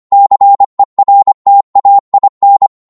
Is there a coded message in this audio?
Yes: Morse code from 0.1 to 2.7 s, heard as a keyed tone, about 25 wpm, 800 Hz. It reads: CERTAIN